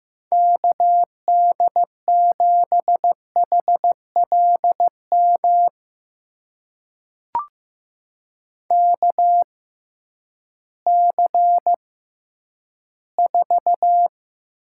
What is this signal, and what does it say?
Morse code, heard as a keyed tone, about 15 wpm, spaced as for 10 wpm, 700 Hz. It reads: KD7HLM KC4
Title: USA Call Signs - 6 Characters - 6x Character Spacing - 15wpm